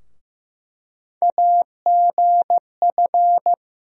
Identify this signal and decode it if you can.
Morse code, heard as a keyed tone, about 15 wpm, 700 Hz. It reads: AGF
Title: USA Call Sign Postfixes - 3 Letters 15wpm